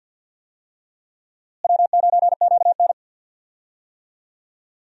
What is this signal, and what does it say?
Morse code, heard as a keyed tone, about 52 wpm, 700 Hz. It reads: W9QN
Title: USA Call Signs - 4 Characters - Repeated 2x - 50wpm